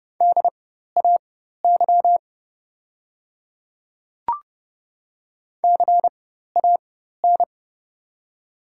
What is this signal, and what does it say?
Morse code, heard as a keyed tone, about 30 wpm, spaced as for 14 wpm, 700 Hz. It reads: DAY CAN
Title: 4x Character Spacing - Top 100 Words - 30wpm